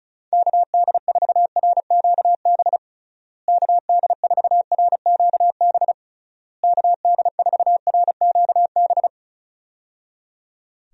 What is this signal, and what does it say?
Morse code, heard as a keyed tone, about 36 wpm, 700 Hz. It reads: KD4RQB KD4RQB KD4RQB
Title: USA Call Signs - 6 Characters - Repeated 3x - 35wpm